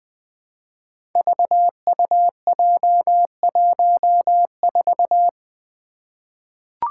Morse code, heard as a keyed tone, about 21 wpm, 700 Hz. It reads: VUJ14
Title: Five Challenging Letter-Number Combinations 20wpm